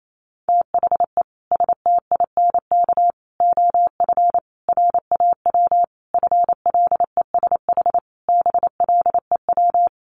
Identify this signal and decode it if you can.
Morse code, heard as a keyed tone, about 28 wpm, 700 Hz. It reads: THE STINK OF RAW FLESH BLEW
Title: A Sound Of Thunder 28wpm